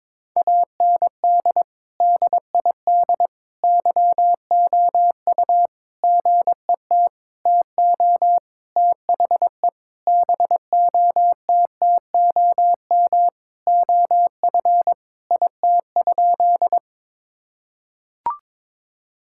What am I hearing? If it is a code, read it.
Morse code, heard as a keyed tone, about 22 wpm, 700 Hz. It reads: AND DID YOU GET TO THE BOTTOM OF IT?
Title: Sentences from Top 1000 Words - Review Part 6 - 22wpm